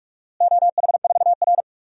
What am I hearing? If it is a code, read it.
Morse code, heard as a keyed tone, about 42 wpm, 700 Hz. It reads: OH4R
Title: DXCC-100 Call Signs - 4 Characters - Repeated 3x - 45wpm